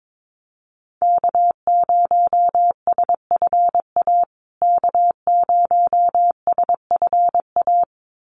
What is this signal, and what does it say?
Morse code, heard as a keyed tone, about 22 wpm, 700 Hz. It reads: K0SFA K0SFA
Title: USA Call Signs Repeated 2x 22wpm